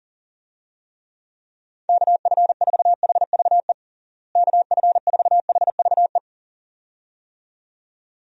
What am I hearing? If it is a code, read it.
Morse code, heard as a keyed tone, about 40 wpm, 700 Hz. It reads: KF4HVE KF4HVE
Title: USA Call Signs - 6 Characters - Repeated 2x - 40wpm